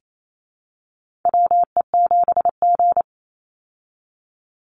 Morse code, heard as a keyed tone, about 28 wpm, 700 Hz. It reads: WE7G
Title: Call Signs Repeated 3x - 28wpm